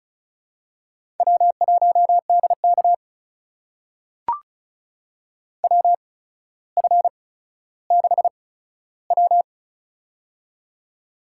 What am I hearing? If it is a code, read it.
Morse code, heard as a keyed tone, about 36 wpm, spaced as for 15 wpm, 700 Hz. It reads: W1DK WF6W
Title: USA Call Signs - 4 Characters - 8x Character Spacing - 35wpm